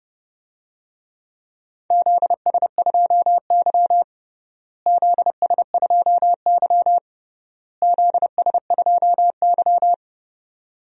Morse code, heard as a keyed tone, about 30 wpm, 700 Hz. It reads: ZS2Y ZS2Y ZS2Y
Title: DXCC-100 Call Signs - 4 Characters - Repeated 3x - 30wpm